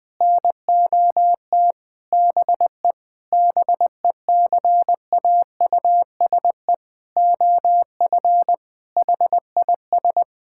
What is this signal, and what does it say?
Morse code, heard as a keyed tone, about 20 wpm, 700 Hz. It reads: NOT BE BECAUSE OF HIS
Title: Sentences from Top 1600 Words 20wpm